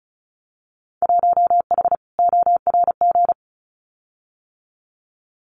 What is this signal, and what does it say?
Morse code, heard as a keyed tone, about 36 wpm, 700 Hz. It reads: 1H ORG
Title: Winter Field Day Exchanges 35wpm